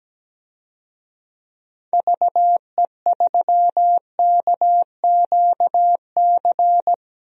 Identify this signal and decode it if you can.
Morse code, heard as a keyed tone, about 17 wpm, 700 Hz. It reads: VE3KQC
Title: DXCC-100 Call Signs - 6 Characters - 17wpm